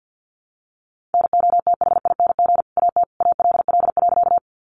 Morse code, heard as a keyed tone, about 50 wpm, 700 Hz. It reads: NOTHING AT ALL.
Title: Sentences from Top 500 Words 50wpm